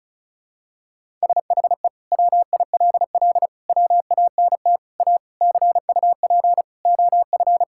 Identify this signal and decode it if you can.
Morse code, heard as a keyed tone, about 36 wpm, 700 Hz. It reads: SHE WILL WANT A CUP OF
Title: Sentences from Top 1600 Words 35wpm